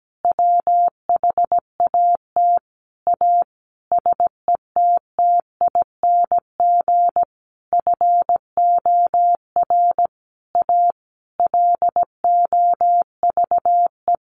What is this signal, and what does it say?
Morse code, heard as a keyed tone, about 17 wpm, 700 Hz. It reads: WHAT A SETTING FOR A LOVE
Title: Sentences from Top 1300 Words 17wpm